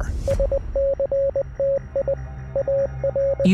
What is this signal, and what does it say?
Morse code, heard as a keyed tone, about 20 wpm, 550 Hz. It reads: SCTIAR